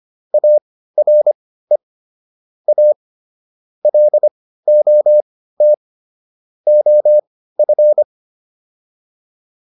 Morse code, heard as a keyed tone, about 25 wpm, spaced as for 15 wpm, 600 Hz. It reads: ARE A LOT OF